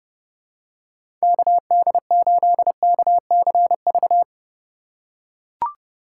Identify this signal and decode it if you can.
Morse code, heard as a keyed tone, about 30 wpm, 700 Hz. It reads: KD8KCV
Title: USA Call Signs - 6 Characters - 4x Character Spacing - 30wpm